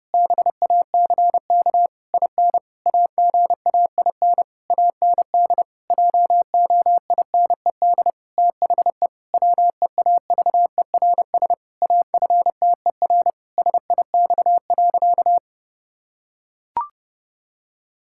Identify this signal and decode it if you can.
Morse code, heard as a keyed tone, about 28 wpm, 700 Hz. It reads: BACK IN AGAIN AND JOINED THE WEAVERS AFTER SIX.
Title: Pangram Challenge v2 - 30wpm